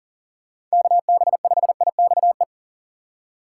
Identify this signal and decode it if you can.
Morse code, heard as a keyed tone, about 40 wpm, 700 Hz. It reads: KB5IXE
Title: USA Call Signs - 6 Characters - 6x Character Spacing - 40wpm